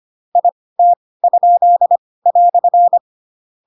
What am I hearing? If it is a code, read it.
Morse code, heard as a keyed tone, about 25 wpm, spaced as for 18 wpm, 700 Hz. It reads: IT?"
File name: Pinocchio / Pinocchio ch6 12 25 ik1hge